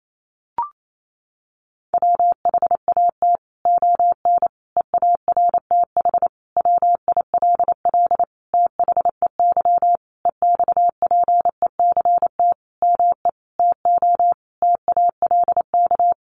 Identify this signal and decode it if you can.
Morse code, heard as a keyed tone, about 28 wpm, 700 Hz. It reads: WHAT ON EARTH WILL THEY EXPECT ME TO TALK
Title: Sentences from Top 1100 Words 28wpm